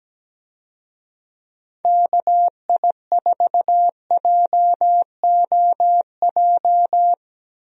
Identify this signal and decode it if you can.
Morse code, heard as a keyed tone, about 17 wpm, 700 Hz. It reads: KI4JOJ
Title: USA Call Signs - 6 Characters - 17wpm